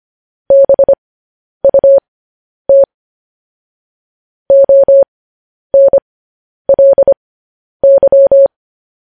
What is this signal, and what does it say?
Morse code, heard as a keyed tone, about 25 wpm, spaced as for 10 wpm, 550 Hz. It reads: BUT ONLY
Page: news headlines in morse code at 10 wpm